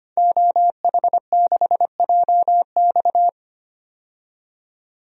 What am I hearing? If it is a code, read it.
Morse code, heard as a keyed tone, about 25 wpm, 700 Hz. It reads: OH6JX
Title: DXCC-100 Call Signs - 5 Characters - 25wpm